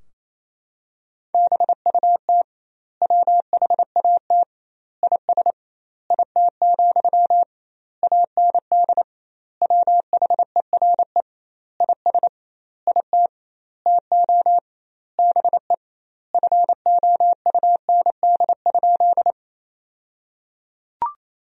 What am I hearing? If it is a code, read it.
Morse code, heard as a keyed tone, about 28 wpm, 700 Hz. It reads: BUT WHAT IS IT, AND WHERE IS IT TO BE FOUND?
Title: 2x Word-Spacing - Sentences from Top 300 Words - 28wpm